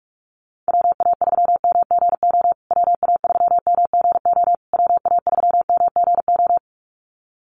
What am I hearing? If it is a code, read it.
Morse code, heard as a keyed tone, about 44 wpm, 700 Hz. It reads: WA3MGO WA3MGO WA3MGO
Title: USA Call Signs Repeated 3x 45wpm